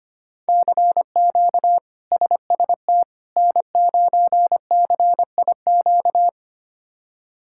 Morse code, heard as a keyed tone, about 25 wpm, 700 Hz. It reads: CQ SST N9CIQ
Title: K1USN SST QSO Practice - 25wpm